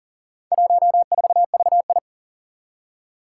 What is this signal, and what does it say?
Morse code, heard as a keyed tone, about 38 wpm, 700 Hz. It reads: 14VI
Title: Four Challenging Letter-Number Combinations 40wpm